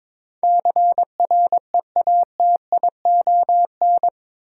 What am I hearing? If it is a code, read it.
Morse code, heard as a keyed tone, about 22 wpm, 700 Hz. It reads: CREATION
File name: practice / 22wpm / Eight Letter Words - 22wpm